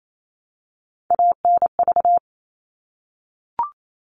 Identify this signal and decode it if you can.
Morse code, heard as a keyed tone, about 28 wpm, 700 Hz. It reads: ANV
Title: All Cut Numbers - 3-Digit 28wpm